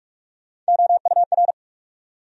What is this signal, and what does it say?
Morse code, heard as a keyed tone, about 60 wpm, spaced as for 53 wpm, 700 Hz. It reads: OUR
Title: Speed-Racing Top 200 Words 45wpm